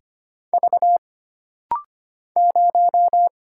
Morse code, heard as a keyed tone, about 25 wpm, 700 Hz. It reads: V 0
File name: Single Character (Letter:Number:Punctuation:Prosign) - Rapid-Fire - 25wpm